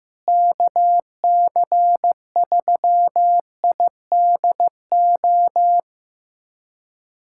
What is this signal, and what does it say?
Morse code, heard as a keyed tone, about 15 wpm, 700 Hz. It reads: KC3IDO
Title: USA Callsign DE Callsign 15wpm